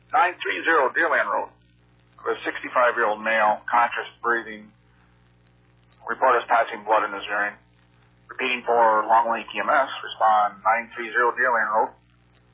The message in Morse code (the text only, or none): none